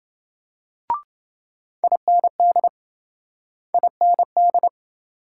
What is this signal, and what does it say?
Morse code, heard as a keyed tone, about 28 wpm, 700 Hz. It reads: IND IND